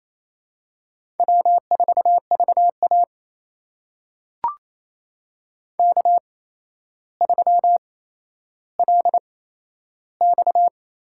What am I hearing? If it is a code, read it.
Morse code, heard as a keyed tone, about 28 wpm, spaced as for 12 wpm, 700 Hz. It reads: W4VA K3LX
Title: USA Call Signs - 4 Characters - 8x Character Spacing - 28wpm